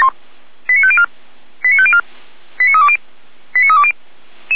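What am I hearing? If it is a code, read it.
Five-tone selective call (ZVEI), heard as quick runs of five tones, tones of about 70 ms. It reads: 89504 89504 89320 89320
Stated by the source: ZVEI five-tone call 89315 received on an SDR.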